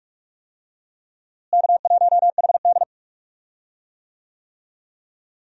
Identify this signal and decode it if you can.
Morse code, heard as a keyed tone, about 46 wpm, 700 Hz. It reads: K1HD
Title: USA Call Signs - 4 Characters - Repeated 3x - 45wpm